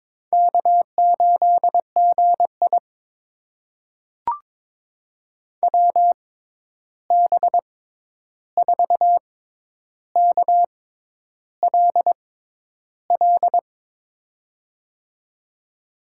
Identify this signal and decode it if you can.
Morse code, heard as a keyed tone, about 22 wpm, spaced as for 10 wpm, 700 Hz. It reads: K8GI WB4KLL
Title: USA Call Signs - 6x Character Spacing - 22wpm